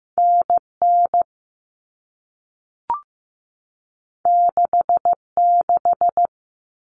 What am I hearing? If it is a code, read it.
Morse code, heard as a keyed tone, about 15 wpm, 700 Hz. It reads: NN 66
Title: Common Cut Numbers - 2-Digit 15wpm